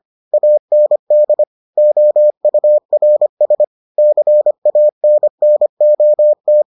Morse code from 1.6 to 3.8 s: OURS

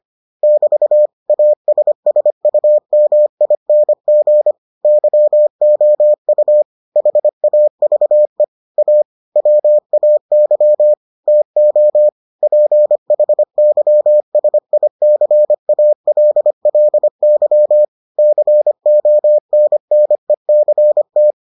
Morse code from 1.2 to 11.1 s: ASSUMING YOU HAVE A WAY

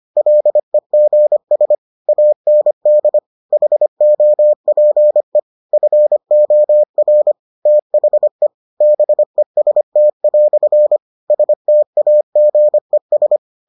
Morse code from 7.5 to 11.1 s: THE BEST"

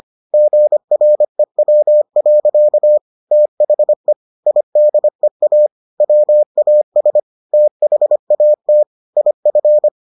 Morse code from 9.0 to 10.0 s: IF

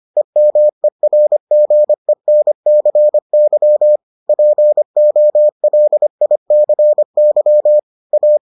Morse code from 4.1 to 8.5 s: POLICY A

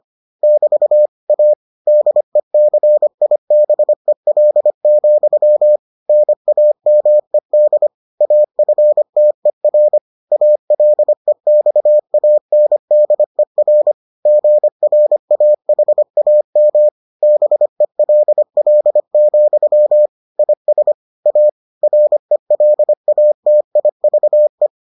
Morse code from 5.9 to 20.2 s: NAMED AFTER ALEXANDER GRAHAM BELL,

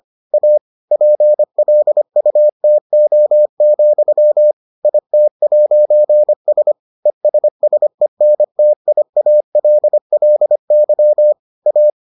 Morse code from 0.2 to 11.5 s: A PLUTO, IT'S ESSENTIALLY

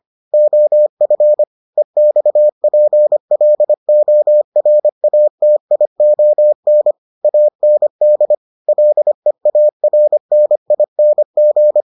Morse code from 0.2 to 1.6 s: OF

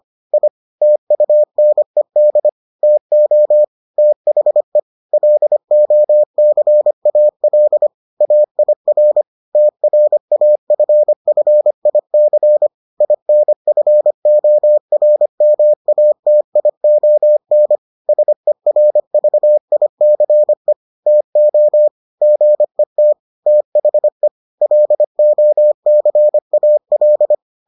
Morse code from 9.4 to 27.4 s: TRAFFIC INFORMATION SERVICE TO GET THE LOCAL

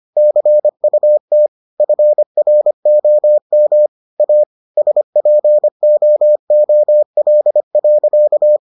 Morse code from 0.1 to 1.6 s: CUT

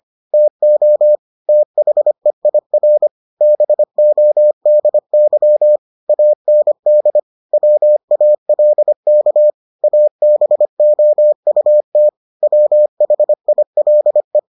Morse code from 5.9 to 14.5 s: AND WALK ABOUT WHILE

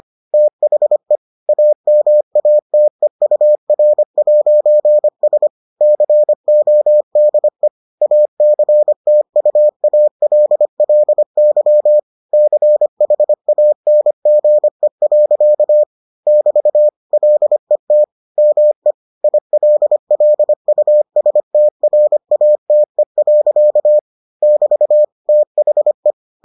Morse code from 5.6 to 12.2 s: CODE ACTUALLY

